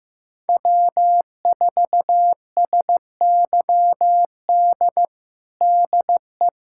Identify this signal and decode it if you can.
Morse code, heard as a keyed tone, about 16 wpm, 700 Hz. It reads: W4SYD DE
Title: USA Callsign DE Callsign - 5 Characters - 15wpm